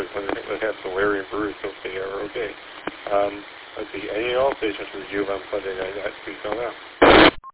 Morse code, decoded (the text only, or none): none